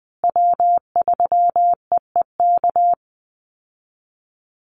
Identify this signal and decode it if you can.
Morse code, heard as a keyed tone, about 20 wpm, 700 Hz. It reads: W3EEK
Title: Call Signs Repeated 3x - 20wpm